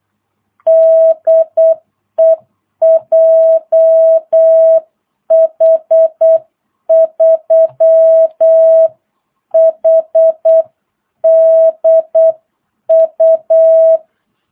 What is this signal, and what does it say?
Morse code, heard as a keyed tone, about 8 wpm, 650 Hz. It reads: DE JH3HDU
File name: jh8cbh / 2022cw